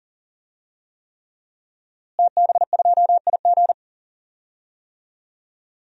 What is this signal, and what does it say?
Morse code, heard as a keyed tone, about 40 wpm, 700 Hz. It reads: TB2IG
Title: DXCC-100 Call Signs - 5 Characters - Repeated 3x - 40wpm